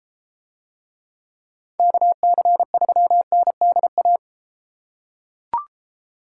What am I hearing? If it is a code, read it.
Morse code, heard as a keyed tone, about 32 wpm, 700 Hz. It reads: KC3NDA